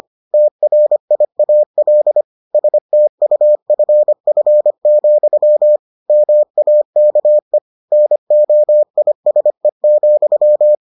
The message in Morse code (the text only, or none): TRIAL STUFF, MAKE NOISE,